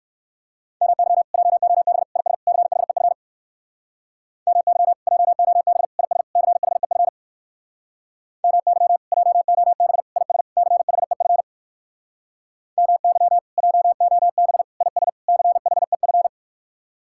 Morse code, heard as a keyed tone, about 59 wpm, 700 Hz. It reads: MY JOB IS CHEF MY JOB IS CHEF MY JOB IS CHEF MY JOB IS CHEF